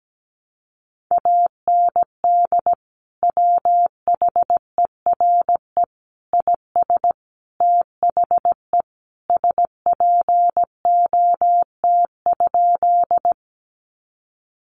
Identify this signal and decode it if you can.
Morse code, heard as a keyed tone, about 17 wpm, 700 Hz. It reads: AND WHERE IS THE SPOT?
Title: Sentences from Top 1100 Words 17wpm